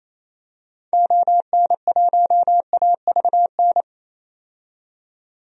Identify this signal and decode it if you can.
Morse code, heard as a keyed tone, about 27 wpm, 700 Hz. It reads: ON1AVN
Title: DXCC-100 Call Signs - 6 Characters - 28wpm